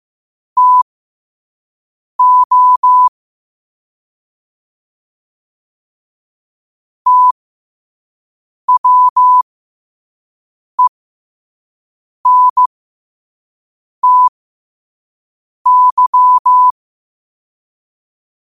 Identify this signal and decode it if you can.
Morse code, heard as a keyed tone, about 15 wpm, spaced as for 5 wpm, 1000 Hz. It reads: TO TWENTY
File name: qotdfiles / 5wpm